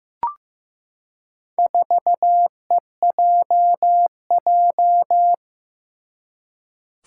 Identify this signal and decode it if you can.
Morse code, heard as a keyed tone, about 15 wpm, 700 Hz. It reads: 4EJJ
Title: Four Challenging Letter-Number Combinations 15wpm